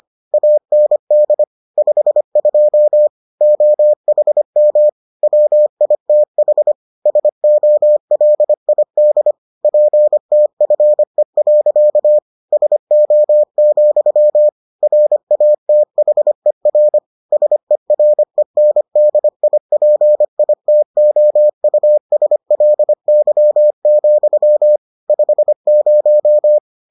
Morse code, heard as a keyed tone, about 25 wpm, 600 Hz. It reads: AND 52 OHM WITH SOLID PTFE. SO, RATHER SERENDIPITOUSLY, 50